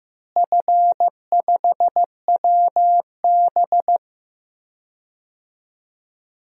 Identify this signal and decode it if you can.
Morse code, heard as a keyed tone, about 14 wpm, 700 Hz. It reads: F5WB
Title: DXCC-100 Call Signs - 4 Characters - Repeated 2x - 15wpm